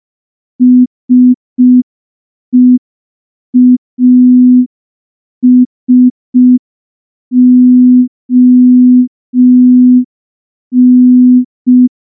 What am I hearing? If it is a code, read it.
Morse code, heard as a keyed tone, about 5 wpm, 250 Hz. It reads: SEASON
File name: morse / morse-training-05WPM